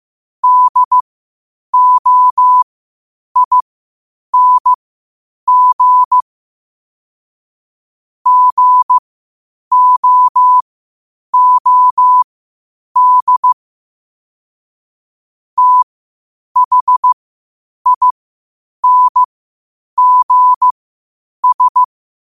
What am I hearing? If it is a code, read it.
Morse code, heard as a keyed tone, about 15 wpm, spaced as for 8 wpm, 1000 Hz. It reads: DOING GOOD THINGS
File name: qotdfiles / 8wpm